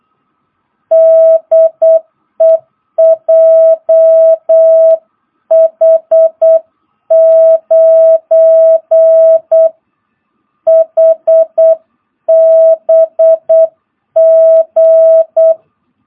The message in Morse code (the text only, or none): DE JH9HBG